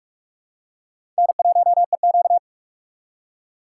Morse code, heard as a keyed tone, about 45 wpm, 700 Hz. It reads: N1EQ
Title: USA Call Signs - 4 Characters - 4x Character Spacing - 45wpm